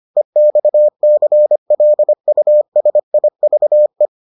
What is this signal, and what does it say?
Morse code, heard as a keyed tone, about 25 wpm, 600 Hz. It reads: EXCLUSIVE